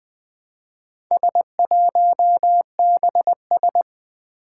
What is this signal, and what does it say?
Morse code, heard as a keyed tone, about 20 wpm, 700 Hz. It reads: S1BS
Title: Four Challenging Letter-Number Combinations 20wpm